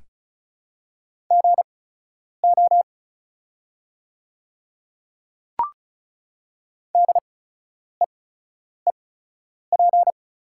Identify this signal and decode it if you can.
Morse code, heard as a keyed tone, about 36 wpm, spaced as for 10 wpm, 700 Hz. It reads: GO DEEP
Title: 8x Character Spacing - Top 500 Words - 35wpm